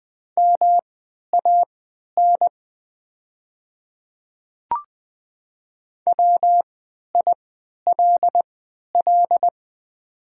Morse code, heard as a keyed tone, about 20 wpm, spaced as for 11 wpm, 700 Hz. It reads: MAN WILL
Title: 3x Character Spacing - Top 300 Words - 20wpm